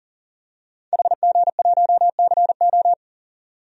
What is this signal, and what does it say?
Morse code, heard as a keyed tone, about 39 wpm, 700 Hz. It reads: HG1CO
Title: DXCC-100 Call Signs - 5 Characters - 40wpm